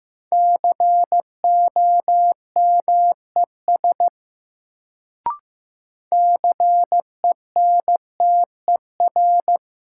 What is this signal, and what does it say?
Morse code, heard as a keyed tone, about 15 wpm, 700 Hz. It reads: COMES CENTER
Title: Sets of 3 Words - New Word Drill - 15wpm